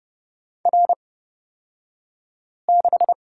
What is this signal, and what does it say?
Morse code, heard as a keyed tone, about 30 wpm, spaced as for 5 wpm, 700 Hz. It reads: R6
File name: Lesson 38d - Letter q - Two Characters 30wpm